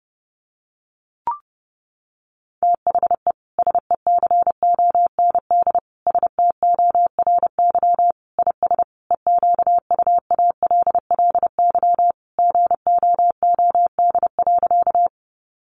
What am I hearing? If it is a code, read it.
Morse code, heard as a keyed tone, about 28 wpm, 700 Hz. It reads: THE SECOND STORY IS EQUALLY GOOD.
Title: Sentences from Top 1400 Words 30wpm